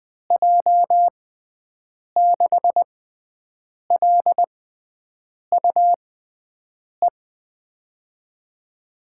Morse code, heard as a keyed tone, about 21 wpm, spaced as for 7 wpm, 700 Hz. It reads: J6LUE
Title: USA Call Signs - 6 Characters - 6x Character Spacing - 20wpm